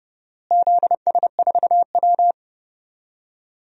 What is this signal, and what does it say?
Morse code, heard as a keyed tone, about 29 wpm, 700 Hz. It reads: ZS4W
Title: DXCC-100 Call Signs - 4 Characters - Repeated 2x - 30wpm